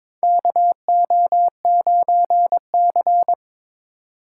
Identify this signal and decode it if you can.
Morse code, heard as a keyed tone, about 22 wpm, 700 Hz. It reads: KO9C
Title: USA Call Signs - 4x Character Spacing - 22wpm